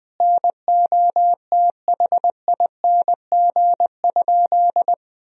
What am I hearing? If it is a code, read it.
Morse code, heard as a keyed tone, about 21 wpm, 700 Hz. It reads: NOTHING?
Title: 4x Word-Spacing - Sentences from Top 500 Words - 20wpm